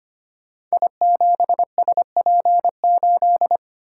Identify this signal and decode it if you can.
Morse code, heard as a keyed tone, about 25 wpm, 700 Hz. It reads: I7SP8
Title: Code Groups - 5 Letter:Numbers - 25wpm